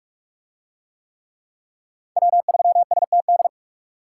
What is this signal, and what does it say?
Morse code, heard as a keyed tone, about 44 wpm, 700 Hz. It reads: W3STD